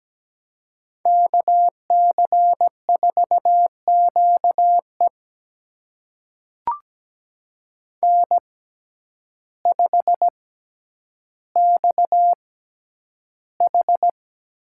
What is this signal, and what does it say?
Morse code, heard as a keyed tone, about 17 wpm, spaced as for 9 wpm, 700 Hz. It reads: KC4QE N5XH
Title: USA Call Signs - 5 Characters - 6x Character Spacing - 17wpm